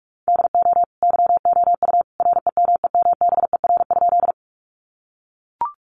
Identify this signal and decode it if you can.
Morse code, heard as a keyed tone, about 44 wpm, 700 Hz. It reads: DO YOU REMEMBER?
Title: Sentences from Top 400 Words 45wpm